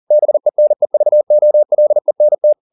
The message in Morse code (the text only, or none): BENEVOLENT